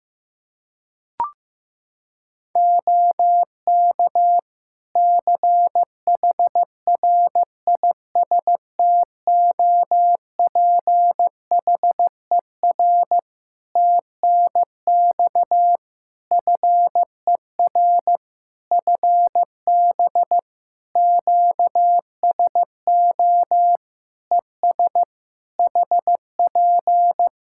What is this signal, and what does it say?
Morse code, heard as a keyed tone, about 16 wpm, 700 Hz. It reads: OK CHRISTOPHER TNX FER FB QSO ES HP
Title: Practice QSO - Exchange 4 - Ending - 15wpm